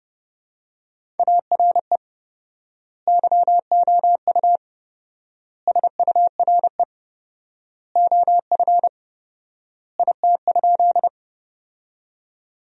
Morse code, heard as a keyed tone, about 30 wpm, 700 Hz. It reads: ARE YOU SURE OF IT?
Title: 4x Word-Spacing - Sentences from Top 300 Words - 30wpm